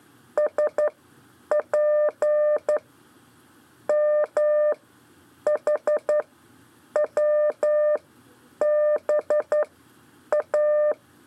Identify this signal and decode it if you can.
Morse code, heard as a keyed tone, about 11 wpm, spaced as for 8 wpm, 600 Hz. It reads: SP M H WB A
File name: 4 LEZIONE CW 1